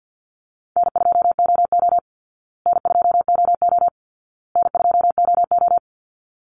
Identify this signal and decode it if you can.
Morse code, heard as a keyed tone, about 50 wpm, 700 Hz. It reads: N2OO N2OO N2OO
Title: Easy Call Signs Repeated 3x - 50wpm